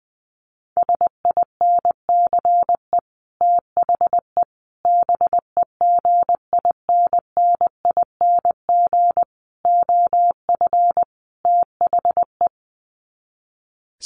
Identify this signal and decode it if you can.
Morse code, heard as a keyed tone, about 20 wpm, 700 Hz. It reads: SINCE THE BEGINNING OF THE
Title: Sets of 5 Words Encore2 20wpm